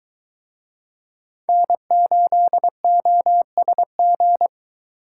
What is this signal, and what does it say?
Morse code, heard as a keyed tone, about 23 wpm, 700 Hz. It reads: N8OSG